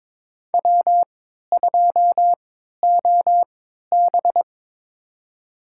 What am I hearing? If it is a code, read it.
Morse code, heard as a keyed tone, about 22 wpm, spaced as for 12 wpm, 700 Hz. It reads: W2OB
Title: USA Call Signs - 4 Characters - 3x Character Spacing - 22wpm